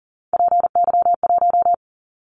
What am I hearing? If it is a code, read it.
Morse code, heard as a keyed tone, about 42 wpm, 700 Hz. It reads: PY1
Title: Callsign Prefixes for Top 50 DXCC - 40wpm